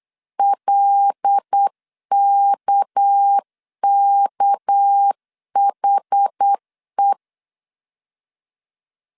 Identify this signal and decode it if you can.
Morse code, heard as a keyed tone, about 8 wpm, 800 Hz. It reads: LKKHE
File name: 8wpm